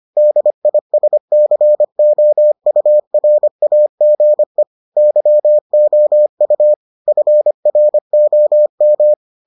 Morse code, heard as a keyed tone, about 25 wpm, 600 Hz. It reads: DISCOURAGE YOU FROM